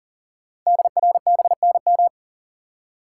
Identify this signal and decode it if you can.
Morse code, heard as a keyed tone, about 42 wpm, 690 Hz. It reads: DRBNM
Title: Code Groups - 5 Letters - 40wpm